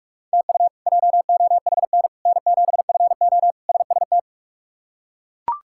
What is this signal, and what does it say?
Morse code, heard as a keyed tone, about 41 wpm, 700 Hz. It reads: TU JOHN N7FO SST
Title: K1USN SST QSO Practice - 45wpm